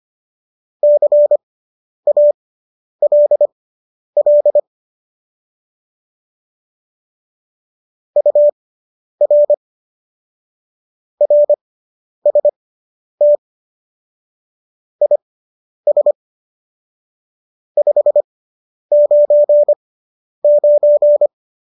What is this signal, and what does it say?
Morse code, heard as a keyed tone, about 25 wpm, spaced as for 10 wpm, 590 Hz. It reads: CALL UR RST IS 599